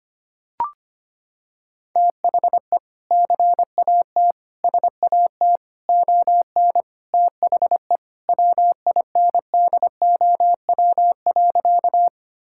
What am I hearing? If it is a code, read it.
Morse code, heard as a keyed tone, about 25 wpm, 700 Hz. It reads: THE CAT SAT ON THE WINDOW.
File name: practice / 25wpm / Sentences with 2-Word Phrases - 25wpm